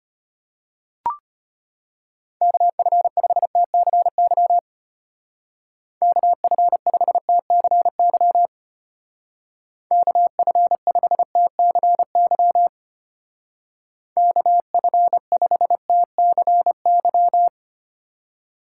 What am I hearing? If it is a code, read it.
Morse code, heard as a keyed tone, about 32 wpm, 700 Hz. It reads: KF5TCY KF5TCY KF5TCY KF5TCY